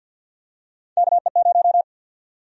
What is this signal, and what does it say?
Morse code, heard as a keyed tone, about 49 wpm, 700 Hz. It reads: KE0
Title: USA Call Sign Prefixes - 50wpm